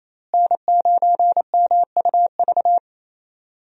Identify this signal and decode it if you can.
Morse code, heard as a keyed tone, about 28 wpm, 700 Hz. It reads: N9MUV
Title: USA Call Signs - 5 Characters - Repeated 2x - 28wpm